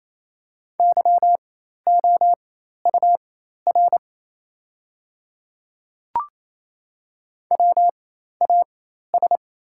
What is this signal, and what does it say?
Morse code, heard as a keyed tone, about 28 wpm, spaced as for 13 wpm, 700 Hz. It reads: YOUR WAS
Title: 4x Character Spacing - Top 100 Words - 28wpm